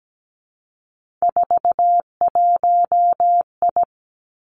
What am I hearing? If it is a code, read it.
Morse code, heard as a keyed tone, about 17 wpm, 700 Hz. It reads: 41I
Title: Three Challenging Letter-Number Combinations 17wpm